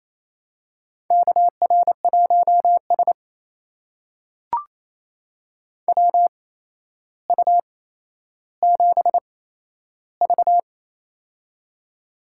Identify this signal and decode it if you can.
Morse code, heard as a keyed tone, about 28 wpm, spaced as for 12 wpm, 700 Hz. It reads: KR1S WU7V
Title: USA Call Signs - 4 Characters - 8x Character Spacing - 28wpm